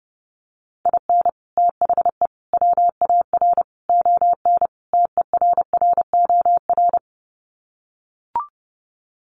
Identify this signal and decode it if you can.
Morse code, heard as a keyed tone, about 30 wpm, 700 Hz. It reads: IN THE WAR ON TERROR
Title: Sets of 5 Words Encore2 30wpm